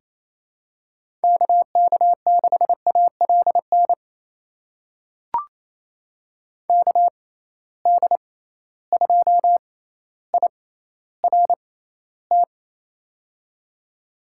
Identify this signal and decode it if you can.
Morse code, heard as a keyed tone, about 28 wpm, spaced as for 14 wpm, 700 Hz. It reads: KK6ALN KD2IRT
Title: USA Call Signs - 6 Characters - 6x Character Spacing - 28wpm